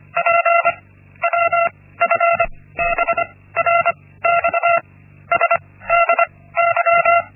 Morse code, heard as a keyed tone, about 25 wpm, 680 Hz. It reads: P W F B R X S D Y
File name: Linksmakalnis-1